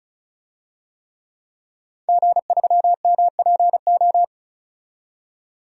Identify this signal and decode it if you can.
Morse code, heard as a keyed tone, about 36 wpm, 700 Hz. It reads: G3MPO
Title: DXCC-100 Call Signs - 5 Characters - 35wpm